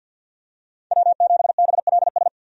Morse code, heard as a keyed tone, about 53 wpm, 700 Hz. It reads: W7BLS
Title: USA Call Signs - 5 Characters - 50wpm